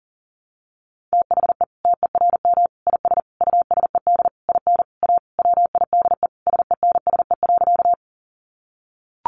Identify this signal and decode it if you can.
Morse code, heard as a keyed tone, about 40 wpm, 700 Hz. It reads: THE TERM IS USED IN A WIDE SENSE.
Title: Sentences from Top 900 Words 40wpm